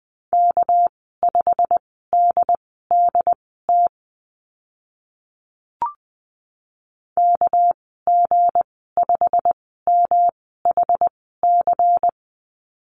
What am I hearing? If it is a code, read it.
Morse code, heard as a keyed tone, about 20 wpm, spaced as for 14 wpm, 700 Hz. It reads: K5DDT KG5MHC
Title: 207 Lesson 33g - Letter j - Call Signs 20wpm